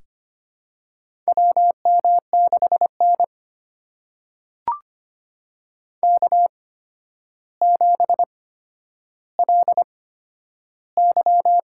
Morse code, heard as a keyed tone, about 25 wpm, spaced as for 11 wpm, 700 Hz. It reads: WM6N K7LY
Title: USA Call Signs - 4 Characters - 8x Character Spacing - 25wpm